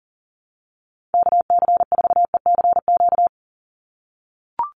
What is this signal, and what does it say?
Morse code, heard as a keyed tone, about 37 wpm, 700 Hz. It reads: KC4ECQ
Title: USA Call Signs 40wpm